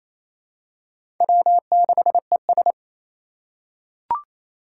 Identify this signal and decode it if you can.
Morse code, heard as a keyed tone, about 27 wpm, 700 Hz. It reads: W6ES
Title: Four Challenging Letter-Number Combinations 28wpm